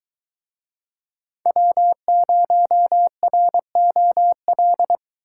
Morse code, heard as a keyed tone, about 23 wpm, 700 Hz. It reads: W0ROL